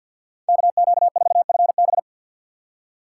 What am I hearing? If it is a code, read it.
Morse code, heard as a keyed tone, about 50 wpm, 700 Hz. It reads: KQ4FB
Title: USA Call Signs - 5 Characters - 4x Character Spacing - 50wpm